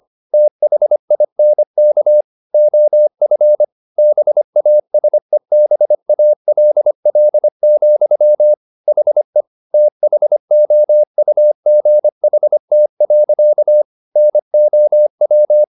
Morse code, heard as a keyed tone, about 25 wpm, 600 Hz. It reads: THINK OF BASEBALL, HE THOUGHT. NOW
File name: the old man and the sea part 1 25wpm